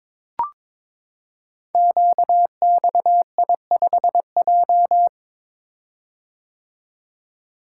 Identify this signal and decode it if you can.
Morse code, heard as a keyed tone, about 22 wpm, 700 Hz. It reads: QXI5J
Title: Code Groups - 5 Letter:Numbers - 22wpm